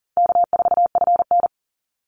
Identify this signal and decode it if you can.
Morse code, heard as a keyed tone, about 40 wpm, 700 Hz. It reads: K4FN
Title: Call Signs Repeated 2x - 40wpm